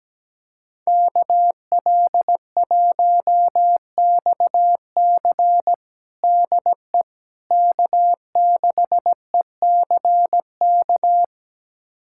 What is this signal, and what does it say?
Morse code, heard as a keyed tone, about 17 wpm, 700 Hz. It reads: KL1XC DE K6ECK
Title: USA Callsign DE Callsign - 5 Characters - 17wpm